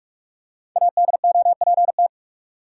Morse code, heard as a keyed tone, about 46 wpm, 700 Hz. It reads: ADOPT
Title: Five Letter Words - 45wpm